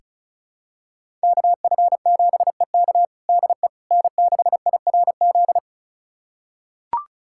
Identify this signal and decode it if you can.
Morse code, heard as a keyed tone, about 34 wpm, 700 Hz. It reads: KF7EK DE N6IRZ
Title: USA Callsign DE Callsign - 5 Characters - 35wpm